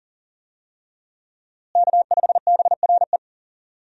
Morse code, heard as a keyed tone, about 37 wpm, 690 Hz. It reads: K5BRE
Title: USA Call Signs - 5 Characters - Repeated 3x - 40wpm